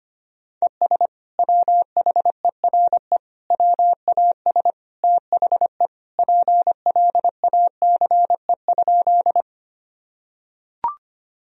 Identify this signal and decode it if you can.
Morse code, heard as a keyed tone, about 25 wpm, 700 Hz. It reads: ES WHERE WAS THE PLACE?
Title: Sentences from Top 200 Words and Abbreviations 25wpm